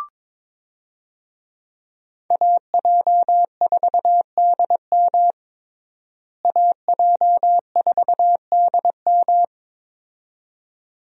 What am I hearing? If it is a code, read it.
Morse code, heard as a keyed tone, about 22 wpm, 700 Hz. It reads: AJ4DM AJ4DM
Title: USA Call Signs - 5 Characters - Repeated 2x - 22wpm